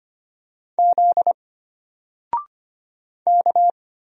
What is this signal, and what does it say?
Morse code, heard as a keyed tone, about 25 wpm, 700 Hz. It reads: Z K
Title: Single Character (Letter:Number:Punctuation:Prosign) Alt 25wpm